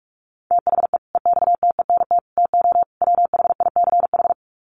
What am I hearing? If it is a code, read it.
Morse code, heard as a keyed tone, about 44 wpm, 700 Hz. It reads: THE EXTENT TO WHICH
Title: Sets of 4 Words 45wpm